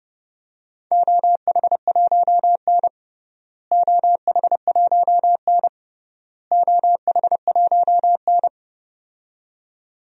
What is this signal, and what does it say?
Morse code, heard as a keyed tone, about 30 wpm, 700 Hz. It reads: OH1N OH1N OH1N
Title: DXCC-100 Call Signs - 4 Characters - Repeated 3x - 30wpm